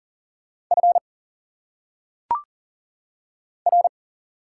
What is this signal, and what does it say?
Morse code, heard as a keyed tone, about 40 wpm, 700 Hz. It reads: F R
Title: Single Letters - Phonetic Pronunciation 40wpm